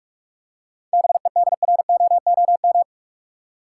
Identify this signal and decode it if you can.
Morse code, heard as a keyed tone, about 42 wpm, 700 Hz. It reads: BEDROOM
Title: Seven Letter Words - 45wpm